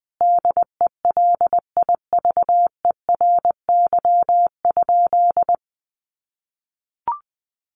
Morse code, heard as a keyed tone, about 20 wpm, 700 Hz. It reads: DELIVERY?
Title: Sentences from Top 1500 Words 20wpm